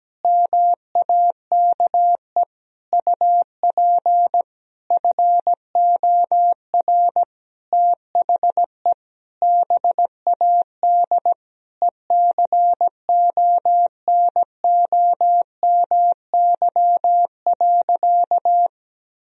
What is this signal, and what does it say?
Morse code, heard as a keyed tone, about 17 wpm, 700 Hz. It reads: MAKE UP FOR THE BAD ECONOMY.
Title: Sentences from Top 1000 Words - Review Part 3 - 17wpm